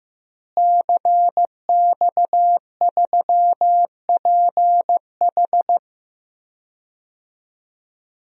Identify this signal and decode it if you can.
Morse code, heard as a keyed tone, about 15 wpm, 700 Hz. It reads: CX3PH
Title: DXCC-100 Call Signs - 5 Characters - Repeated 3x - 15wpm